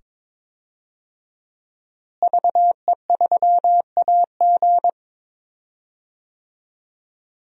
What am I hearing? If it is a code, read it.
Morse code, heard as a keyed tone, about 22 wpm, 700 Hz. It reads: VE3AG
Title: DXCC-100 Call Signs - 5 Characters - Repeated 2x - 22wpm